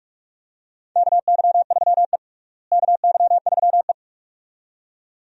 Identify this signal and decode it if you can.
Morse code, heard as a keyed tone, about 43 wpm, 700 Hz. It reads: KY3E KY3E
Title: USA Call Signs - 4 Characters - Repeated 2x - 45wpm